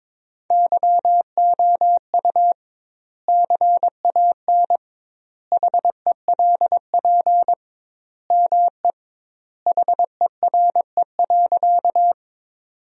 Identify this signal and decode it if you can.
Morse code, heard as a keyed tone, about 22 wpm, 700 Hz. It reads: YOU CAN HELP ME HERE.
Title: 2x Word-Spacing - Sentences from Top 200 Words - 22wpm